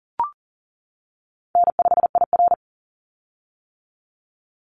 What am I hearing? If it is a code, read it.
Morse code, heard as a keyed tone, about 40 wpm, 700 Hz. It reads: N5IR
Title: Easy Call Signs - 40wpm